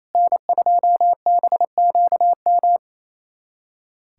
Morse code, heard as a keyed tone, about 28 wpm, 700 Hz. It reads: N2BQM
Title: USA Callsign DE Callsign - 5 Characters - 28wpm